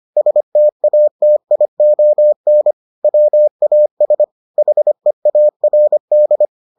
Morse code, heard as a keyed tone, about 25 wpm, 600 Hz. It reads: STATION WAS HEARD